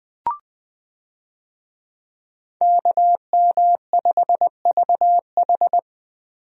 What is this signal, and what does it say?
Morse code, heard as a keyed tone, about 20 wpm, 700 Hz. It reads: KM5VH
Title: USA Call Signs - 5 Characters - 20wpm